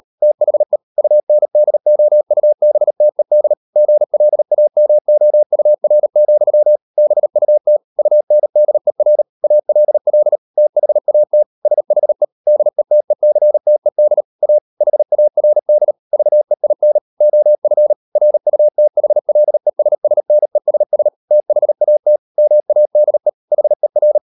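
Morse code, heard as a keyed tone, about 38 wpm, 600 Hz. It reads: THE UNDOUBTED GLAMOUR, BUT UNDER ALL THAT SHE DETECTED A HARD VEIN OF RUTHLESSNESS THAT MADE HER
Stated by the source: Morse code practice at 38 WPM, sending Ken Follett's novel Never in English, from CW med Gnister.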